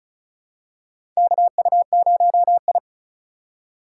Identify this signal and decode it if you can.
Morse code, heard as a keyed tone, about 36 wpm, 700 Hz. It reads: KU0I
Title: USA Call Signs - 4 Characters - 2x Character Spacing - 35wpm